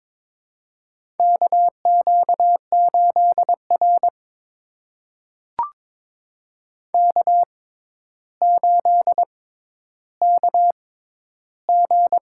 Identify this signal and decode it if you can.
Morse code, heard as a keyed tone, about 22 wpm, spaced as for 11 wpm, 700 Hz. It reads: KQ8R K8KG
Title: USA Call Signs - 4 Characters - 6x Character Spacing - 22wpm